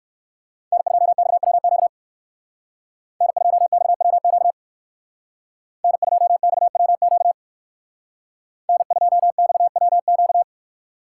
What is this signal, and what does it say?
Morse code, heard as a keyed tone, about 58 wpm, 700 Hz. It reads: N2XWQ N2XWQ N2XWQ N2XWQ